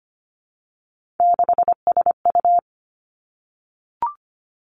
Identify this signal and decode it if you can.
Morse code, heard as a keyed tone, about 25 wpm, 700 Hz. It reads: 6SU
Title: Three Challenging Letter-Number Combinations 25wpm